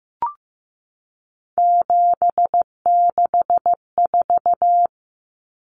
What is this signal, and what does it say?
Morse code, heard as a keyed tone, about 15 wpm, 700 Hz. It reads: 764
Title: Numbers - 3-Digit 15wpm